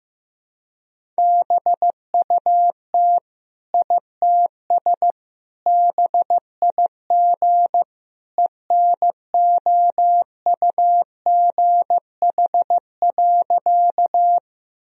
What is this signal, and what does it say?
Morse code, heard as a keyed tone, about 15 wpm, 700 Hz. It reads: BUT ITS BIG ENOUGH.
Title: Sentences from Top 1000 Words - Review Part 1 - 15wpm